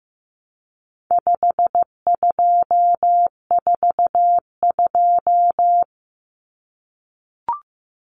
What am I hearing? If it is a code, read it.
Morse code, heard as a keyed tone, about 15 wpm, 700 Hz. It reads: 5242
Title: Numbers - 4-Digit 15wpm